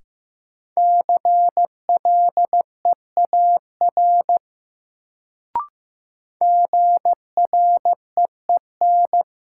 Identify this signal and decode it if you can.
Morse code, heard as a keyed tone, about 15 wpm, 700 Hz. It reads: CLEAR GREEN
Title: Sentences from Top 500 Words - New Word Drill - 15wpm